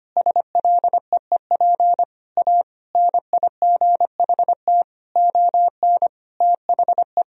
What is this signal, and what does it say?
Morse code, heard as a keyed tone, about 25 wpm, 700 Hz. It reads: SLEEP A NIGHT ON THE
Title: Sentences from Top 1000 Words - Review Part 2 - 25wpm